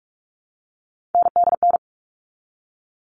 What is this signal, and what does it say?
Morse code, heard as a keyed tone, about 44 wpm, 700 Hz. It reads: NDN
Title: All Cut Numbers - 3-Digit 45wpm